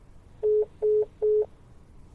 There is no code.